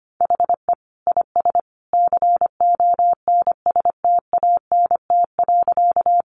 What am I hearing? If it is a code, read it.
Morse code, heard as a keyed tone, about 25 wpm, 700 Hz. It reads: HE IS CONSTANT.
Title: Sentences from Top 1500 Words 25wpm